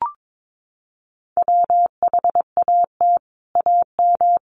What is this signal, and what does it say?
Morse code, heard as a keyed tone, about 22 wpm, 700 Hz. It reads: WHAT AM
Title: Sets of 5 Words Encore2 22wpm